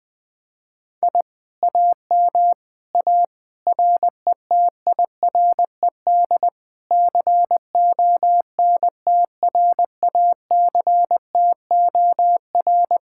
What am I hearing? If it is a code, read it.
Morse code, heard as a keyed tone, about 20 wpm, 700 Hz. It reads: I AM A RETIRED CONTRACTOR
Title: Speed-Racing Common QSO Phrases - 17wpm